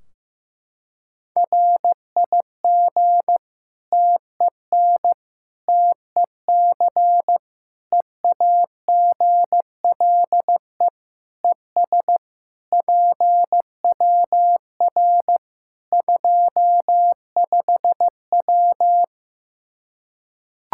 Morse code, heard as a keyed tone, about 15 wpm, 700 Hz. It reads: RIG TEN TEC EAGLE ES PWR 25W
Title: K1USN SST QSO Practice 15wpm